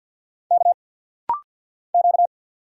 Morse code, heard as a keyed tone, about 45 wpm, 700 Hz. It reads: K Q
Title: Single Letters - Rapid-Fire 50wpm